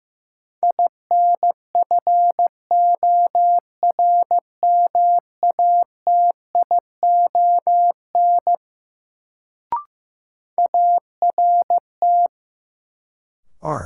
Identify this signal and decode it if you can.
Morse code, heard as a keyed tone, about 15 wpm, 700 Hz. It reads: INFORMATION ART
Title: Sentences from Top 600 Words - New Word Drill - 15wpm